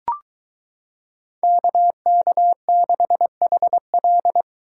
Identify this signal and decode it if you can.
Morse code, heard as a keyed tone, about 23 wpm, 700 Hz. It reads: KK6HL